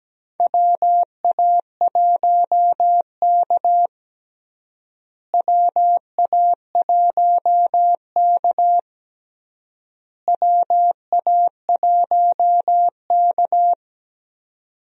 Morse code, heard as a keyed tone, about 17 wpm, 700 Hz. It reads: WA1K WA1K WA1K
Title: USA Call Signs - 4 Characters - Repeated 3x - 17wpm